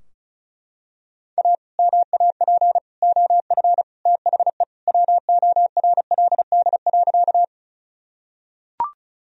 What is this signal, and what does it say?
Morse code, heard as a keyed tone, about 36 wpm, 700 Hz. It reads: A MAP OF THE WORLD.
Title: Sentences from Top 1000 Words - Review Part 2 - 35wpm